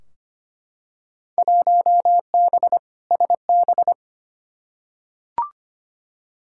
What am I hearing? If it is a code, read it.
Morse code, heard as a keyed tone, about 25 wpm, 700 Hz. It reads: 1B SB
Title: Field Day Exchanges - 25wpm